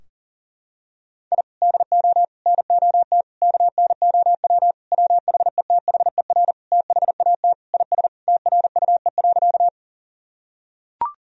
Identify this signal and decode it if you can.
Morse code, heard as a keyed tone, about 40 wpm, 700 Hz. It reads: I DO NOT KNOW WHETHER THAT IS TRUE.
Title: Sentences from Top 1000 Words - Review Part 3 - 40wpm